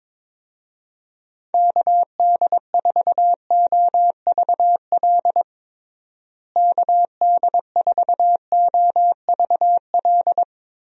Morse code, heard as a keyed tone, about 22 wpm, 700 Hz. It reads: KD4OVL KD4OVL
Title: USA Call Signs - 6 Characters - Repeated 3x - 22wpm